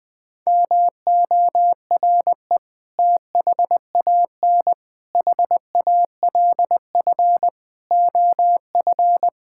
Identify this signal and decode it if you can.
Morse code, heard as a keyed tone, about 20 wpm, 700 Hz. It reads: MORE THAN HALF OF
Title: Sentences with 5-Word Phrases - 20wpm